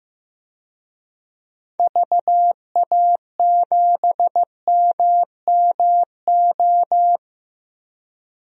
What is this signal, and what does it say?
Morse code, heard as a keyed tone, about 16 wpm, 700 Hz. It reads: VA7MMO
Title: DXCC-100 Call Signs - 6 Characters - 15wpm